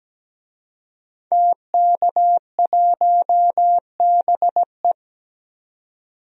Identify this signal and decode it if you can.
Morse code, heard as a keyed tone, about 17 wpm, 700 Hz. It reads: TK1BE
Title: DXCC-100 Call Signs - 5 Characters - 17wpm